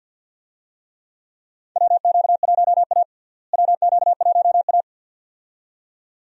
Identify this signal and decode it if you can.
Morse code, heard as a keyed tone, about 52 wpm, 700 Hz. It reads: WQ1A WQ1A
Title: USA Call Signs - 4 Characters - Repeated 2x - 50wpm